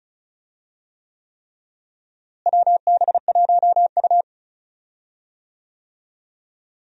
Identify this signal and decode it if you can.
Morse code, heard as a keyed tone, about 36 wpm, 700 Hz. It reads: WB1U
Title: USA Call Signs - 4 Characters - Repeated 3x - 35wpm